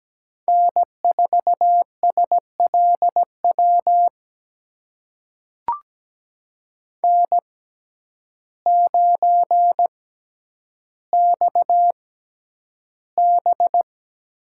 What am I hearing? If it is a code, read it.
Morse code, heard as a keyed tone, about 17 wpm, spaced as for 9 wpm, 700 Hz. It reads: N4SLW N9XB
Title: USA Call Signs - 6x Character Spacing - 17wpm